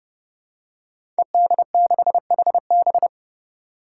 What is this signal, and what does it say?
Morse code, heard as a keyed tone, about 30 wpm, 700 Hz. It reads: ED6HB
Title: Five Challenging Letter-Number Combinations 30wpm